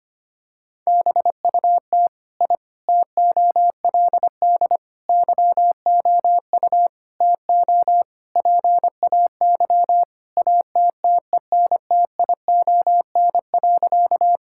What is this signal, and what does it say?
Morse code, heard as a keyed tone, about 25 wpm, 700 Hz. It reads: BUT I TOLD YOU TO PAY ATTENTION.
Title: Sentences from Top 1000 Words - Review Part 6 - 25wpm